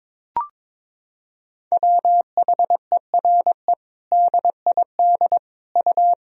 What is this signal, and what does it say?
Morse code, heard as a keyed tone, about 22 wpm, 700 Hz. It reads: WHERE DID U
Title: Sentences from Top 200 Words and Abbreviations 22wpm